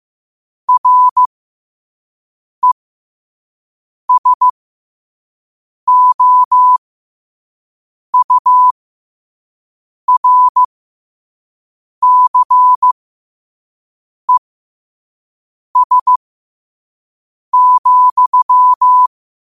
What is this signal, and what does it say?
Morse code, heard as a keyed tone, about 15 wpm, spaced as for 5 wpm, 1000 Hz. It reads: RESOURCES,
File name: qotdfiles / 5wpm